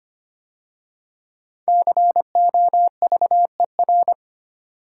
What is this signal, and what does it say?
Morse code, heard as a keyed tone, about 25 wpm, 700 Hz. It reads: COVER
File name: 5 Letter Words - First 3 Letters Given - 25wpm